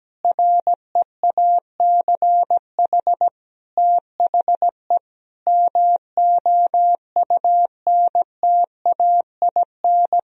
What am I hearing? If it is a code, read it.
Morse code, heard as a keyed tone, about 17 wpm, 700 Hz. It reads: REACH THE MOUNTAIN
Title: Sentences from Top 1000 Words - Review Part 4 - 17wpm